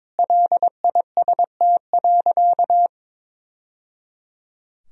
Morse code, heard as a keyed tone, about 22 wpm, 700 Hz. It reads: LIST.
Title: Sentences from Top 1000 Words - Review Part 2 - 22wpm